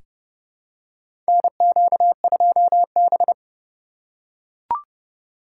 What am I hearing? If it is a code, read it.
Morse code, heard as a keyed tone, about 29 wpm, 700 Hz. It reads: NQ2B
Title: USA Call Signs - 4 Characters - 6x Character Spacing - 30wpm